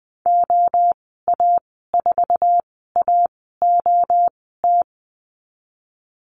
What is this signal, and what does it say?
Morse code, heard as a keyed tone, about 20 wpm, 700 Hz. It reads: O A 4 A O T
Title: 062 Lesson 11g - Letter h - Call Signs 20wpm